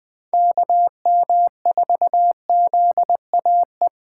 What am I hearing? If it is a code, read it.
Morse code, heard as a keyed tone, about 20 wpm, 700 Hz. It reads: KM4ZAE
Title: USA Call Signs - 6 Characters - 4x Character Spacing - 20wpm